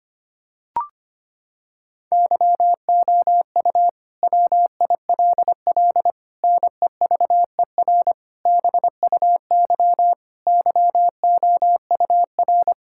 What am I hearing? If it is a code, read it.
Morse code, heard as a keyed tone, about 25 wpm, 700 Hz. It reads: YOU WILL NEVER BUY YOUR
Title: Sentences from Top 1000 Words - Review Part 5 - 25wpm